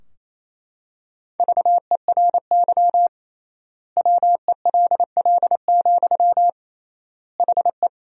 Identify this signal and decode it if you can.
Morse code, heard as a keyed tone, about 28 wpm, 700 Hz. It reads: VERY WELL, HE
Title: 3x Word-Spacing - Sentences from Top 200 Words - 28wpm